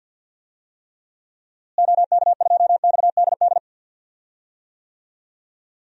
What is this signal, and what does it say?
Morse code, heard as a keyed tone, about 46 wpm, 700 Hz. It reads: OK2XDD
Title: DXCC-100 Call Signs - 6 Characters - Repeated 2x - 50wpm